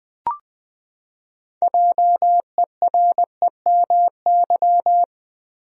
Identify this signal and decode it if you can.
Morse code, heard as a keyed tone, about 20 wpm, 700 Hz. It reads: JEREMY
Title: USA Common Names - 20wpm